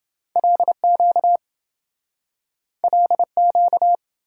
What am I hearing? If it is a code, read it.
Morse code, heard as a keyed tone, about 27 wpm, 700 Hz. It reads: LQ LQ